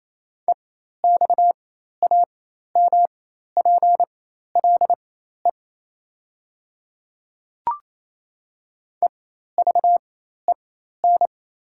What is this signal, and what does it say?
Morse code, heard as a keyed tone, about 28 wpm, spaced as for 13 wpm, 700 Hz. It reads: EXAMPLE EVEN
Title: 4x Character Spacing - Top 300 Words - 28wpm